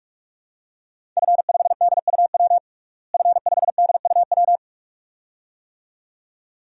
Morse code, heard as a keyed tone, about 46 wpm, 700 Hz. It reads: F5DUW F5DUW
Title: DXCC-100 Call Signs - 5 Characters - Repeated 2x - 45wpm